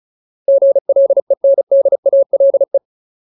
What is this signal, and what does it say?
Morse code, heard as a keyed tone, about 34 wpm, 550 Hz. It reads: GLENDALE
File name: Ultimate Word List / 35WPM Ultimate